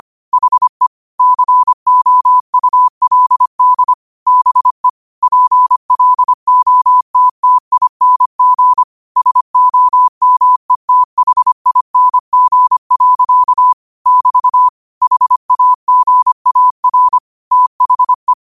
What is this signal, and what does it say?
Morse code, heard as a keyed tone, about 25 wpm, 1000 Hz. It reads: HE COULD BE PLOTTING SOMETHING. <BT> HAGAR THE